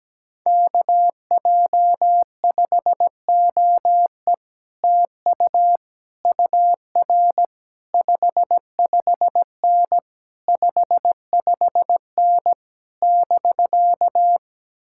Morse code, heard as a keyed tone, about 17 wpm, 700 Hz. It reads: KJ5OE TU UR 55N 55N <BK>
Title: Speed-Racing Common QSO Phrases - 17wpm